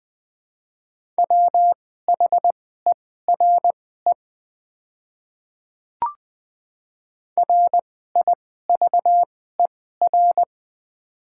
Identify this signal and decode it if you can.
Morse code, heard as a keyed tone, about 20 wpm, spaced as for 14 wpm, 700 Hz. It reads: WHERE RIVER
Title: 2x Character Spacing - Top 300 Words - 20wpm